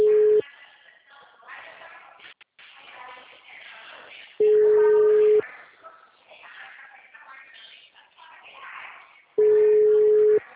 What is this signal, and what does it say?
Ringback tone (Europe, 425 Hz).